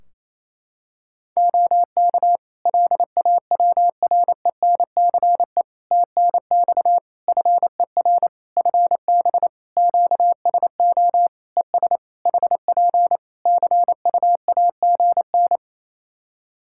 Morse code, heard as a keyed tone, about 28 wpm, 700 Hz. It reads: OK LAWRENCE TNX FER FB QSO ES HP CUAGN
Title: Practice QSO - Exchange 4 - Ending - 28wpm